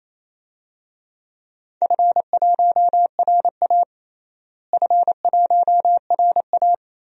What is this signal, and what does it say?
Morse code, heard as a keyed tone, about 28 wpm, 700 Hz. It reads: F1RA F1RA
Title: DXCC-100 Call Signs - 4 Characters - Repeated 3x - 28wpm